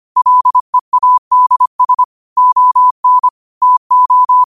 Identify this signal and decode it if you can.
Morse code, heard as a keyed tone, about 25 wpm, 1000 Hz. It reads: LEADS ON TO
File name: qotdfiles / 25wpm